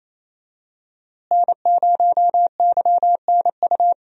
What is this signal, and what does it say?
Morse code, heard as a keyed tone, about 28 wpm, 700 Hz. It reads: N0YNU
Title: USA Call Signs - 5 Characters - Repeated 2x - 28wpm